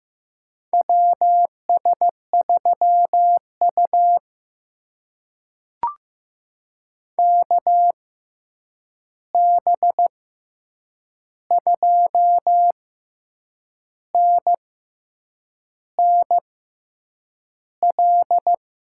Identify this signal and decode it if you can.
Morse code, heard as a keyed tone, about 15 wpm, spaced as for 7 wpm, 700 Hz. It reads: WS3U KB2NNL
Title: USA Call Signs - 6x Character Spacing - 15wpm